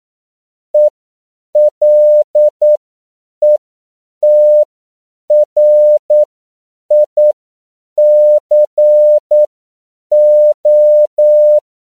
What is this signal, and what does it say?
Morse code, heard as a keyed tone, about 9 wpm, spaced as for 7 wpm, 600 Hz. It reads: ELETRICO
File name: CW / 10PPM